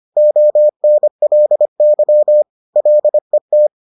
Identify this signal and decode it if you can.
Morse code, heard as a keyed tone, about 25 wpm, 600 Hz. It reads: ONLY LET